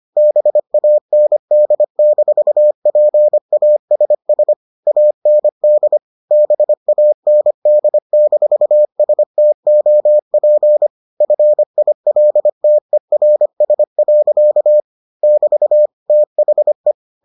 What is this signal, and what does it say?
Morse code, heard as a keyed tone, about 25 wpm, 600 Hz. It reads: BAND-PASS AND BAND-STOP FILTERS. <BT> THE